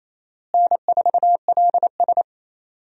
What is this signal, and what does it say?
Morse code, heard as a keyed tone, about 28 wpm, 700 Hz. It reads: N4LS
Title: USA Call Signs - 4 Characters - Repeated 3x - 28wpm